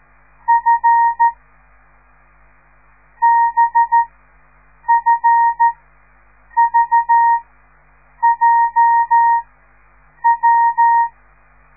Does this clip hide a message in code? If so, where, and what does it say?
Yes: Morse code from 0.5 to 11.1 s, heard as a keyed tone, about 13 wpm, spaced as for 7 wpm, 930 Hz. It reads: F BFVJW